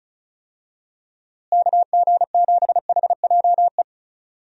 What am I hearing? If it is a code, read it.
Morse code, heard as a keyed tone, about 34 wpm, 700 Hz. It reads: KG7HJE
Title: USA Call Signs - 6 Characters - Repeated 3x - 35wpm